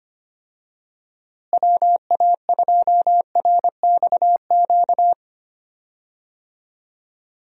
Morse code, heard as a keyed tone, about 25 wpm, 700 Hz. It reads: WA2RXQ
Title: USA Call Signs - 6 Characters - Repeated 2x - 25wpm